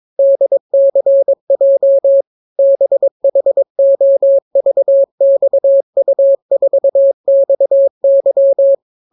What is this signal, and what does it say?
Morse code, heard as a keyed tone, about 22 wpm, 550 Hz. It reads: DCJ BHOVXU4XY